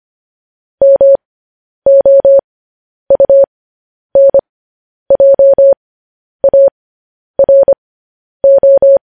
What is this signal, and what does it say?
Morse code, heard as a keyed tone, about 25 wpm, spaced as for 10 wpm, 550 Hz. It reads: MOUNJARO
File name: podcast / News headlines-1x-25x10